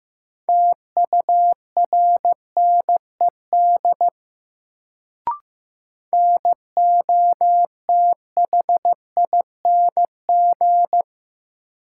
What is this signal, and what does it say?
Morse code, heard as a keyed tone, about 15 wpm, 700 Hz. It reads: TURNED NOTHING
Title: Sets of 4 Words - New Word Drill - 15wpm